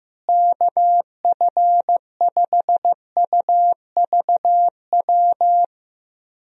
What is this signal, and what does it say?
Morse code, heard as a keyed tone, about 15 wpm, 700 Hz. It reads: KF5UVW
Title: USA Call Signs - 6 Characters - Repeated 2x - 15wpm